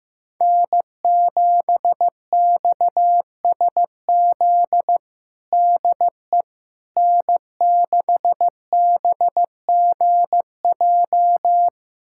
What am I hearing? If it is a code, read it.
Morse code, heard as a keyed tone, about 15 wpm, 700 Hz. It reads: N7XSZ DE N6BGJ
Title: USA Callsign DE Callsign - 5 Characters - 15wpm